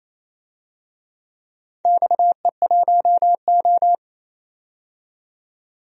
Morse code, heard as a keyed tone, about 28 wpm, 700 Hz. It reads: XE1O
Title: DXCC-100 Call Signs - 4 Characters - 28wpm